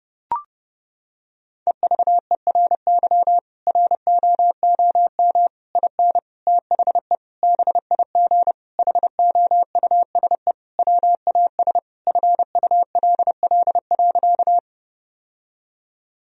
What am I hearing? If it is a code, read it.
Morse code, heard as a keyed tone, about 30 wpm, 700 Hz. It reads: EVERY ROOM IN THE BIG HOUSE WAS FULL.
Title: Sentences from Top 1000 Words - Review Part 3 - 30wpm